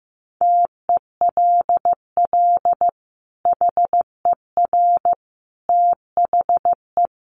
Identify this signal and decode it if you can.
Morse code, heard as a keyed tone, about 14 wpm, 700 Hz. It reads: TELL HER THE
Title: Sentences from Top 800 Words 15wpm